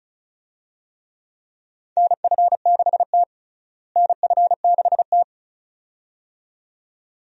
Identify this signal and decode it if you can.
Morse code, heard as a keyed tone, about 34 wpm, 700 Hz. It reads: NF6T NF6T
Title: USA Call Signs - 4 Characters - Repeated 2x - 35wpm